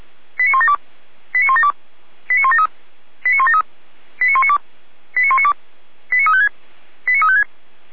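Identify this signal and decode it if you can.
Five-tone selective call (ZVEI), heard as quick runs of five tones, tones of about 70 ms. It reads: 89172 89172 89173 89173 89192 89192 89356 89356
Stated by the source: ZVEI five-tone call received by SDR, archived by Feuerwehr Orlamünde (fire brigade).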